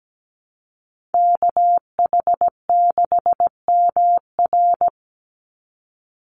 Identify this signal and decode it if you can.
Morse code, heard as a keyed tone, about 17 wpm, 700 Hz. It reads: KH6MR
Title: USA Call Signs 17wpm